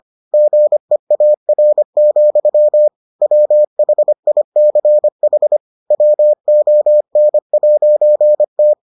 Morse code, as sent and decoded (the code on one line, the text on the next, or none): --. .- .-. --..-- / .-- .... .. -.-. .... / .-- --- -. .----. -
GEAR, WHICH WON'T